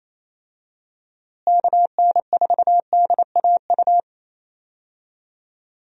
Morse code, heard as a keyed tone, about 28 wpm, 700 Hz. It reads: KN4DAU
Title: USA Call Signs - 6 Characters - Repeated 2x - 28wpm